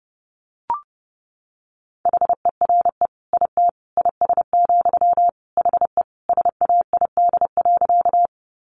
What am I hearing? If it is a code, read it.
Morse code, heard as a keyed tone, about 30 wpm, 700 Hz. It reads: HERE IT IS, HE SAID.
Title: Sentences from Top 200 Words 30wpm